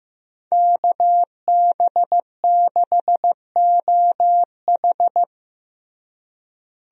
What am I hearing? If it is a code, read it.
Morse code, heard as a keyed tone, about 16 wpm, 700 Hz. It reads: KB6OH
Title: USA Call Signs - 5 Characters - Repeated 2x - 15wpm